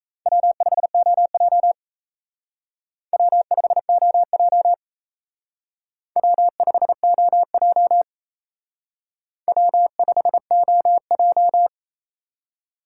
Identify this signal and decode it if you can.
Morse code, heard as a keyed tone, about 35 wpm, 700 Hz. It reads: W5OJ W5OJ W5OJ W5OJ
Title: Speed Racing - USA 1x2 Call Signs 28wpm